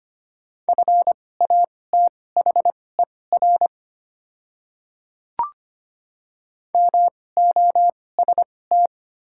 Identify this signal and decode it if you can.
Morse code, heard as a keyed tone, about 25 wpm, spaced as for 18 wpm, 700 Hz. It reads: FATHER MOST
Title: 2x Character Spacing - Top 200 Words - 25wpm